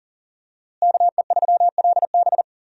Morse code, heard as a keyed tone, about 40 wpm, 700 Hz. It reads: KE3LB
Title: USA Call Signs - 5 Characters - Repeated 3x - 40wpm